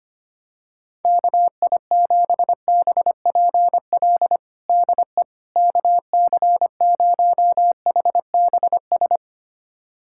Morse code, heard as a keyed tone, about 25 wpm, 700 Hz. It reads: KI7BPL DE KC0HBS
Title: USA Callsign DE Callsign - 6 Characters - 25wpm